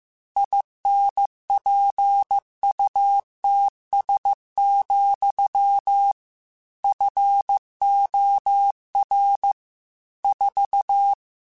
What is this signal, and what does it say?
Morse code, heard as a keyed tone, about 15 wpm, 800 Hz. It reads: INPUTS, FOR 4
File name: Technical Subjects / Technical Subjects-mp3s / AND Gate @13